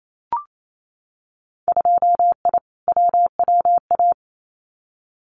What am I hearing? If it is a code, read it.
Morse code, heard as a keyed tone, about 28 wpm, 700 Hz. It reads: 2I WWA
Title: Winter Field Day Exchanges Encore 28wpm